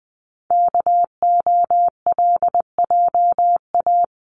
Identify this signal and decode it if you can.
Morse code, heard as a keyed tone, about 20 wpm, 700 Hz. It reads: KOLJA
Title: CWT Exchange 20wpm